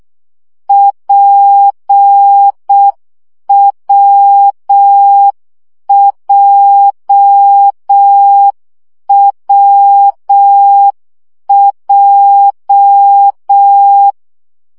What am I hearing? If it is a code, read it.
Morse code, heard as a keyed tone, about 6 wpm, 800 Hz. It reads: PWJWJ
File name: CW gyakorlas w p j 5 baud